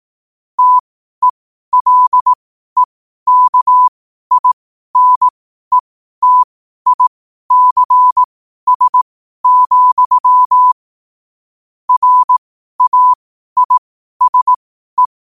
Morse code, heard as a keyed tone, about 18 wpm, spaced as for 12 wpm, 1000 Hz. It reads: TELEKINETICS, RAISE
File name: qotdfiles / 12wpm